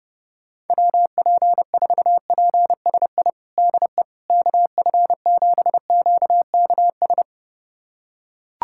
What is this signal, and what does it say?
Morse code, heard as a keyed tone, about 30 wpm, 700 Hz. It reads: WP4PSI DE KF7QKS
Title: USA Callsign DE Callsign - 6 Characters - 30wpm